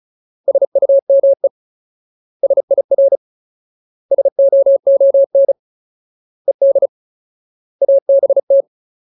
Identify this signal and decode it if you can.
Morse code, heard as a keyed tone, about 34 wpm, 550 Hz. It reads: SUME SIR SOON ED ABT
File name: Ultimate Word List / 35WPM Ultimate